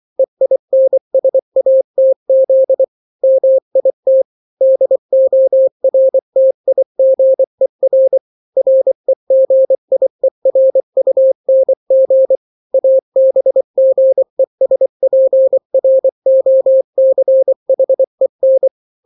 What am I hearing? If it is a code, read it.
Morse code, heard as a keyed tone, about 24 wpm, 530 Hz. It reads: EINSATZ MIT DORTIGER REGIERUNG ABGESPROCHEN
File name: daily / 23wpm